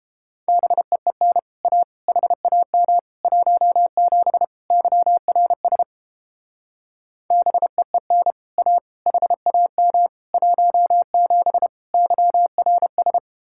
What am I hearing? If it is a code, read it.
Morse code, heard as a keyed tone, about 31 wpm, 700 Hz. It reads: BEEN A HAM 17 YRS BEEN A HAM 17 YRS